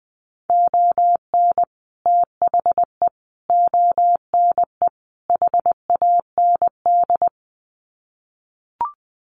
Morse code, heard as a keyed tone, about 20 wpm, 700 Hz. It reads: ON THE ONE HAND
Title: Sets of 4 Words 20wpm